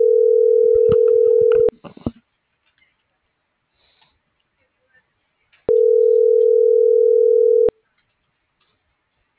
Ringback tone (North America), 440 + 480 Hz.
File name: CENTRAL SOTOVA / out alejandro.rodriguez - .48118